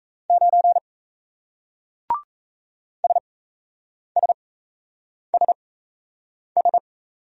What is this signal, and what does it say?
Morse code, heard as a keyed tone, about 37 wpm, spaced as for 8 wpm, 700 Hz. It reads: 9 SSSS